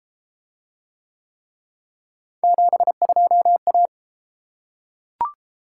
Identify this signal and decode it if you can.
Morse code, heard as a keyed tone, about 30 wpm, 700 Hz. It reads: 72A